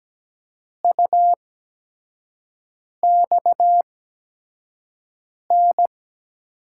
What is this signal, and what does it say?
Morse code, heard as a keyed tone, about 17 wpm, spaced as for 5 wpm, 700 Hz. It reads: UXN
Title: USA Call Signs - 5 Characters - 8x Character Spacing - 17wpm